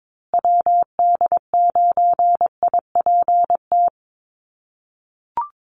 Morse code, heard as a keyed tone, about 22 wpm, 690 Hz. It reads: WD9IPT
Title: USA Call Signs Repeated 3x 22wpm